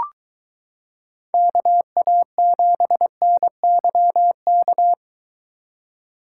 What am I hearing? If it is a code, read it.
Morse code, heard as a keyed tone, about 23 wpm, 700 Hz. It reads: KA7NYK